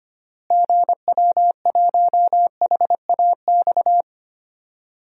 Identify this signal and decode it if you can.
Morse code, heard as a keyed tone, about 25 wpm, 700 Hz. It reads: GW1HAX
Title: DXCC-100 Call Signs - 6 Characters - Repeated 3x - 25wpm